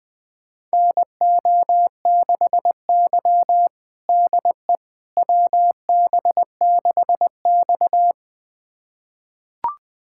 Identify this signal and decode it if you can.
Morse code, heard as a keyed tone, about 20 wpm, 700 Hz. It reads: NO6Y DE WB6X
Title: USA Callsign DE Callsign - 4 Characters - 20wpm